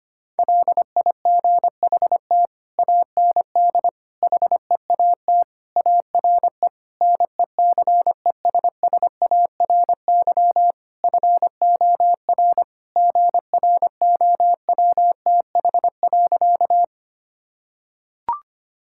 Morse code, heard as a keyed tone, about 25 wpm, 700 Hz. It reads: LIGHT AND HEAT ARE NECESSARY FOR GROWTH.
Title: Sentences from Top 1000 Words - Review Part 5 - 25wpm